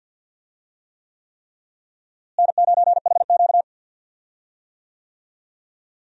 Morse code, heard as a keyed tone, about 52 wpm, 700 Hz. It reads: N9HQ
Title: USA Call Signs - 4 Characters - Repeated 3x - 50wpm